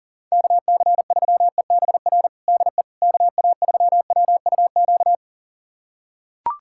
Morse code, heard as a keyed tone, about 40 wpm, 700 Hz. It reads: KC3EBR DE KA3WUQ
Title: USA Callsign DE Callsign - 6 Characters - 40wpm